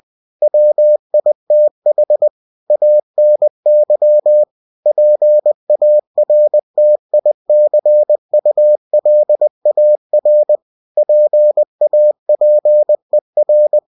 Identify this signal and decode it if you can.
Morse code, heard as a keyed tone, about 20 wpm, 600 Hz. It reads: WITH ANY PARTICULAR PAPER
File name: cw-books / winnie-the-pooh / winnie-20wpm-page-0122